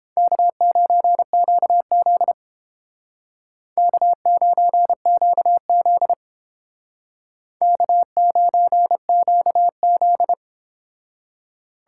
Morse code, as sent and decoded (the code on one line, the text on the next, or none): -.- ----. --.- --.. / -.- ----. --.- --.. / -.- ----. --.- --..
K9QZ K9QZ K9QZ